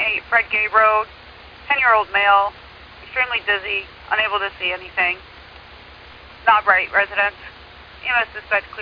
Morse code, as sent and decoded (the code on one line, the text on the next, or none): none